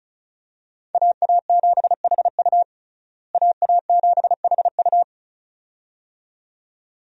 Morse code, heard as a keyed tone, about 34 wpm, 700 Hz. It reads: AA7HU AA7HU
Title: USA Call Signs - 5 Characters - Repeated 2x - 35wpm